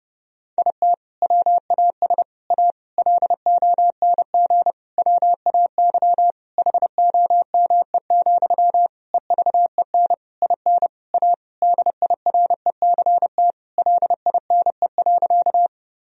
Morse code, heard as a keyed tone, about 30 wpm, 700 Hz. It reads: IT WAS A LONG WAY HOME, EVEN IN A DIRECT LINE.
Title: Sentences from Top 1000 Words - Review Part 2 - 30wpm